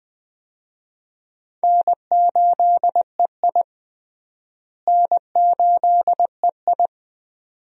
Morse code, heard as a keyed tone, about 20 wpm, 700 Hz. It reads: N8EI N8EI
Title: USA Call Signs - 4 Characters - Repeated 2x - 20wpm